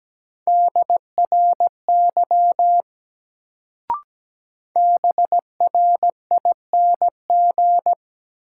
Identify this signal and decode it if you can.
Morse code, heard as a keyed tone, about 17 wpm, 700 Hz. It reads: DRY BRING
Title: Sentences from Top 500 Words - New Word Drill - 17wpm